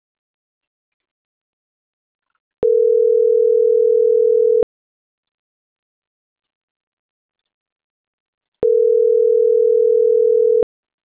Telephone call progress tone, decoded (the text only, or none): ringback tone